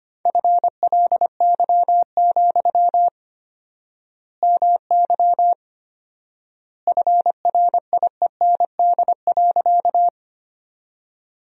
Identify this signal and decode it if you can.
Morse code, heard as a keyed tone, about 25 wpm, 700 Hz. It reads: FLY, MY FRIEND.
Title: 4x Word-Spacing - Sentences from Top 500 Words - 25wpm